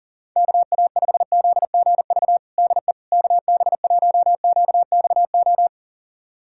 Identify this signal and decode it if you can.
Morse code, heard as a keyed tone, about 37 wpm, 700 Hz. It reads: KA5ZGV DE KB1QXO
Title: USA Callsign DE Callsign - 6 Characters - 40wpm